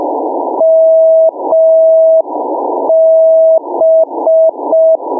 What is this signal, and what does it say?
Morse code, heard as a keyed tone, about 5 wpm, 650 Hz. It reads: MB
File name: mw / 0437 499 mb beacon